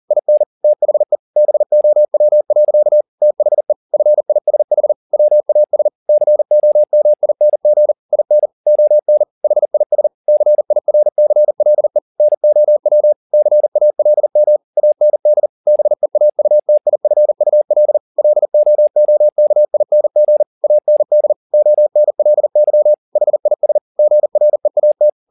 Morse code, heard as a keyed tone, about 40 wpm, 600 Hz. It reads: IN THE BOW. THE FISH WAS COMING IN ON HIS CIRCLE NOW CALM AND BEAUTIFUL LOOKING AND ONLY HIS GREAT
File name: the old man and the sea 2 40wpm